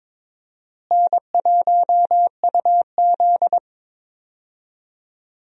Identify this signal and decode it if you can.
Morse code, heard as a keyed tone, about 22 wpm, 700 Hz. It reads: N1UZ
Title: USA Call Signs - 4 Characters - Repeated 2x - 22wpm